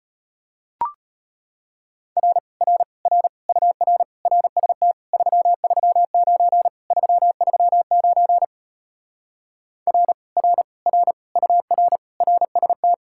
Morse code, heard as a keyed tone, about 33 wpm, 700 Hz. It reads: R R R UR RST 339 339 R R R UR RST